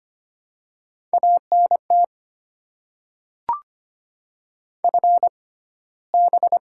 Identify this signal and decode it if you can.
Morse code, heard as a keyed tone, about 25 wpm, spaced as for 15 wpm, 700 Hz. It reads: ANT FB
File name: Top 20 QSO Elements - 6x Character Spacing - 25wpm